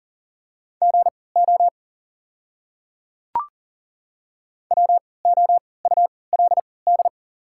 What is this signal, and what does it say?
Morse code, heard as a keyed tone, about 40 wpm, spaced as for 23 wpm, 690 Hz. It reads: GO WOULD
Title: 3x Character Spacing - Top 100 Words - 40wpm